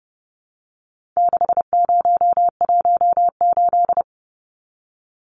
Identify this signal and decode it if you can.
Morse code, heard as a keyed tone, about 30 wpm, 700 Hz. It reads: 6018
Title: Numbers - 4-Digit 30wpm